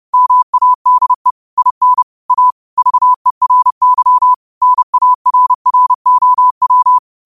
Morse code, heard as a keyed tone, about 30 wpm, 1000 Hz. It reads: MADE IN A VERY NARROW
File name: qotdfiles / 30wpm